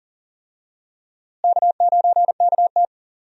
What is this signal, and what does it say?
Morse code, heard as a keyed tone, about 40 wpm, 700 Hz. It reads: K9KT
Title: USA Call Signs - 4 Characters - Repeated 2x - 40wpm